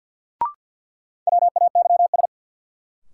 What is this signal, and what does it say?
Morse code, heard as a keyed tone, about 47 wpm, 700 Hz. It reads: WAYS
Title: Sets of 5 Words Encore - New Word Drill - 50wpm